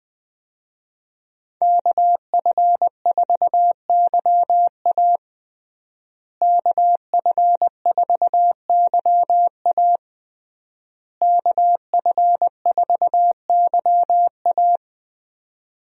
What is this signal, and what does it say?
Morse code, heard as a keyed tone, about 20 wpm, 700 Hz. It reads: KF4YA KF4YA KF4YA
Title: USA Call Signs - 5 Characters - Repeated 3x - 20wpm